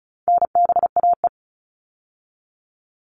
Morse code, heard as a keyed tone, about 34 wpm, 700 Hz. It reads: NBAE
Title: All Cut Numbers - 4-Digit 35wpm